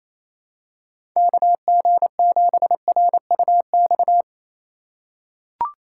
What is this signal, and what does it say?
Morse code, heard as a keyed tone, about 28 wpm, 700 Hz. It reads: KG7RUX
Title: USA Call Signs - 6 Characters - 3x Character Spacing - 28wpm